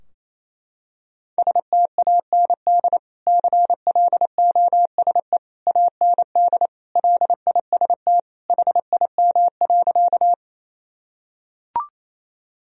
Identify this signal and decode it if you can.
Morse code, heard as a keyed tone, about 28 wpm, 700 Hz. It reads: STAND CLOSE AND LIST HIM.
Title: Sentences from Top 1000 Words - Review Part 2 - 28wpm